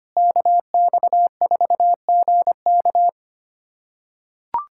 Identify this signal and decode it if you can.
Morse code, heard as a keyed tone, about 25 wpm, 700 Hz. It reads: KX4GK
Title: USA Callsign DE Callsign - 5 Characters - 25wpm